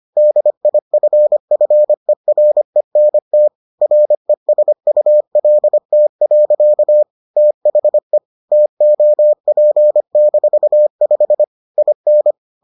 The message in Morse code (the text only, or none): DIFFERENT RESULT. THE TOP-5 IN